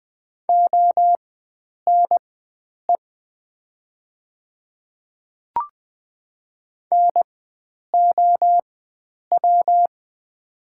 Morse code, heard as a keyed tone, about 20 wpm, spaced as for 9 wpm, 700 Hz. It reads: ONE NOW
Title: 4x Character Spacing - Top 100 Words - 20wpm